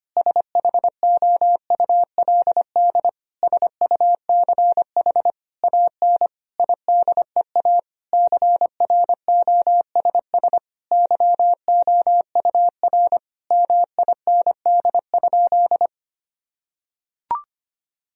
Morse code, heard as a keyed tone, about 25 wpm, 700 Hz. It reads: SHOULD SUCH AN IDEA CROSS YOUR MIND?
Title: Sentences from Top 1000 Words - Review Part 1 - 25wpm